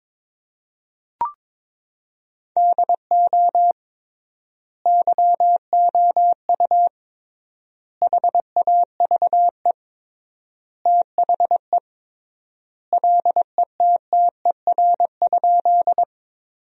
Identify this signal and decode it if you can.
Morse code, heard as a keyed tone, about 22 wpm, 700 Hz. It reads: DO YOU HAVE THE LETTER?
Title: 3x Word-Spacing - Sentences from Top 300 Words - 22wpm